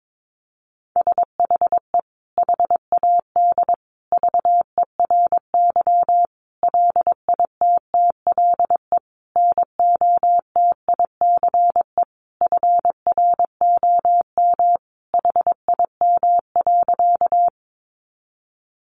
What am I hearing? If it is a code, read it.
Morse code, heard as a keyed tone, about 22 wpm, 700 Hz. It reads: SHE HAD VERY LITTLE NOTICE FROM HIM.
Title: Sentences from Top 400 Words 22wpm